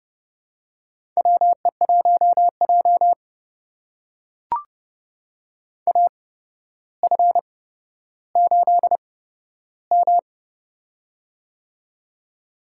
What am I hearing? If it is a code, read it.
Morse code, heard as a keyed tone, about 30 wpm, spaced as for 13 wpm, 700 Hz. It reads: WE1J AF8M
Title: USA Call Signs - 4 Characters - 8x Character Spacing - 30wpm